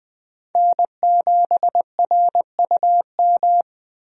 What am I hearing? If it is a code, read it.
Morse code, heard as a keyed tone, about 20 wpm, 700 Hz. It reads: N7RUM
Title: USA Call Signs - 5 Characters - Repeated 2x - 20wpm